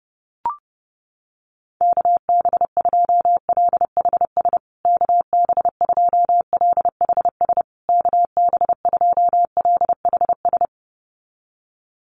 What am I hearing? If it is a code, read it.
Morse code, heard as a keyed tone, about 30 wpm, 700 Hz. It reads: KB2LHS KB2LHS KB2LHS
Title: USA Call Signs Repeated 3x 30wpm